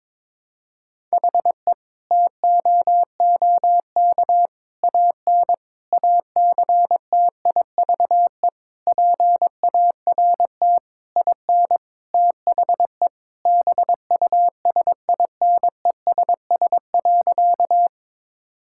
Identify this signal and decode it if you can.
Morse code, heard as a keyed tone, about 22 wpm, 700 Hz. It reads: HE TOOK AN ACTIVE PART IN THE BUSINESS.
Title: Sentences from Top 1000 Words - Review Part 6 - 22wpm